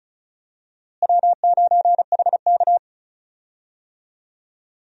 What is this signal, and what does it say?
Morse code, heard as a keyed tone, about 36 wpm, 700 Hz. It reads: W9HK
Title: USA Call Signs - 4 Characters - Repeated 2x - 35wpm